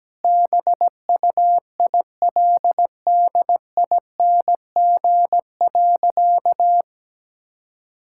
Morse code, heard as a keyed tone, about 17 wpm, 700 Hz. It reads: BUILDING.
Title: Sentences from Top 1000 Words - Review Part 5 - 17wpm